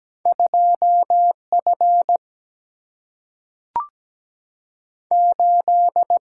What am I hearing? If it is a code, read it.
Morse code, heard as a keyed tone, about 17 wpm, 700 Hz. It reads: 2F 8
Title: Single-Letter Single-Number 17wpm